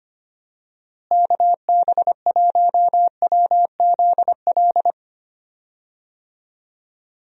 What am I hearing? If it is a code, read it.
Morse code, heard as a keyed tone, about 25 wpm, 700 Hz. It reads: KB1WZL
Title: USA Call Signs - 6 Characters - Repeated 2x - 25wpm